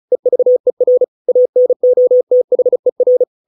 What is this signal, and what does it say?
Morse code, heard as a keyed tone, about 34 wpm, 500 Hz. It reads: EVER ANOTHER